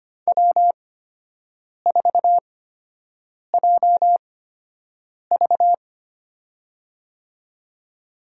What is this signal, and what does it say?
Morse code, heard as a keyed tone, about 25 wpm, spaced as for 7 wpm, 700 Hz. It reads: W4JV
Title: USA Call Signs - 4 Characters - 8x Character Spacing - 25wpm